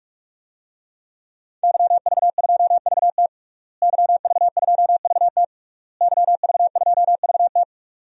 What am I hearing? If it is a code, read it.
Morse code, heard as a keyed tone, about 46 wpm, 700 Hz. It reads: YV2VT YV2VT YV2VT
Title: DXCC-100 Call Signs - 5 Characters - Repeated 3x - 45wpm